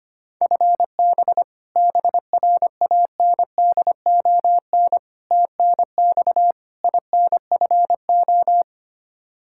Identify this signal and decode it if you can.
Morse code, heard as a keyed tone, about 25 wpm, 700 Hz. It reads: FB BRANDON TNX INFO
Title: Practice QSO - Exchange 3 - More Info - 25wpm